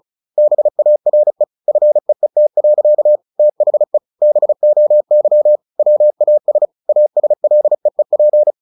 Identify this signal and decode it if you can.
Morse code, heard as a keyed tone, about 34 wpm, 600 Hz. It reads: BARE FEET. THE BOY WAS ASLEEP
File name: the old man and the sea / the old man and the sea 1 35wpm